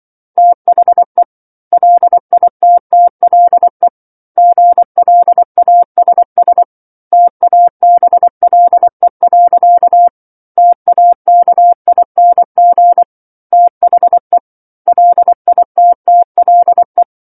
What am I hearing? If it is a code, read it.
Morse code, heard as a keyed tone, about 24 wpm, 700 Hz. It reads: THE LITTLE GLASS TABLE. TAKING THE LITTLE
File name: 07-Chapter-07 24 WPM